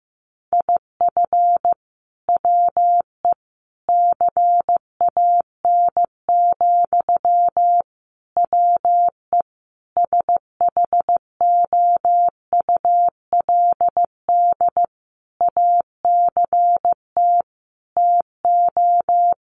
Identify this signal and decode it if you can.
Morse code, heard as a keyed tone, about 15 wpm, 700 Hz. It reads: IF WE CAN, WE SHOULD ACT TO
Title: Sentences from Top 1100 Words 15wpm